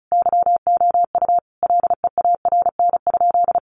Morse code, heard as a keyed tone, about 34 wpm, 700 Hz. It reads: YOU LEARN?
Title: Sentences from Top 1200 Words 35wpm